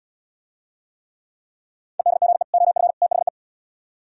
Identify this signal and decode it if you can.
Morse code, heard as a keyed tone, about 76 wpm, 700 Hz. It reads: EDGE OF THE